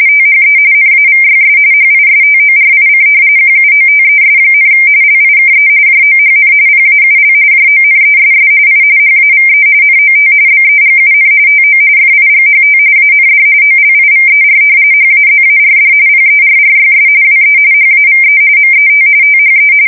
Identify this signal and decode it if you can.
RTTY, about 45.45 baud. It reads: THE NAVYS HISTORIC QUOTES, LIKE JOHN PAUL JONESS I HAVE NOT BEGUN TO FIGHT AND OLIVER HAZARD PERRYS WE HAVE MET THE
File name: fsk170 / battleshipnj